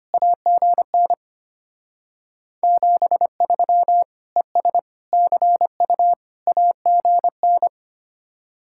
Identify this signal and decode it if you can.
Morse code, heard as a keyed tone, about 31 wpm, 700 Hz. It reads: AGN 73 ES CU AGN